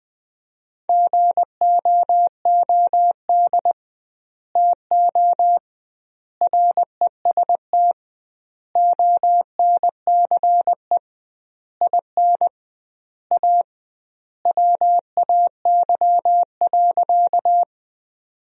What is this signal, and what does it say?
Morse code, heard as a keyed tone, about 20 wpm, 700 Hz. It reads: GOOD TO REST ONCE IN A WAY.
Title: 2x Word-Spacing - Sentences from Top 500 Words - 20wpm